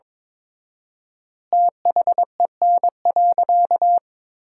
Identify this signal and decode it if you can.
Morse code, heard as a keyed tone, about 22 wpm, 700 Hz. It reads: THEN.
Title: 4x Word-Spacing - Sentences from Top 200 Words - 22wpm